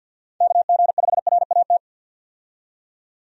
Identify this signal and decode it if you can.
Morse code, heard as a keyed tone, about 53 wpm, 700 Hz. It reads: KG5RAT
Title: USA Call Signs - 6 Characters - 50wpm